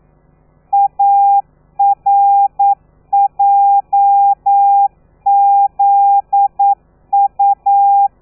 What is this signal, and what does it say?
Morse code, heard as a keyed tone, about 9 wpm, 800 Hz. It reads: ARJZU
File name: telegrafia / i2tfj / Lez13 40